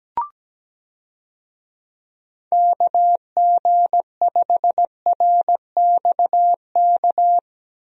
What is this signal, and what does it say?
Morse code, heard as a keyed tone, about 17 wpm, 700 Hz. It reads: KG5RXK
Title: USA Call Signs - 6 Characters - 17wpm